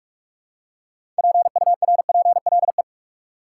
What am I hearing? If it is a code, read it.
Morse code, heard as a keyed tone, about 46 wpm, 700 Hz. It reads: PURPLE
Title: Six Letter Words - 45wpm